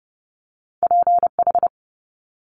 Morse code, heard as a keyed tone, about 30 wpm, 700 Hz. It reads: PH
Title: Multi-letter Phonemes Quick Fire 30wpm